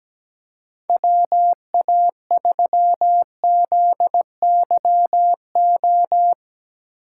Morse code, heard as a keyed tone, about 17 wpm, 700 Hz. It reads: WA3ZYO
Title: USA Call Signs - 6 Characters - 8x Character Spacing - 17wpm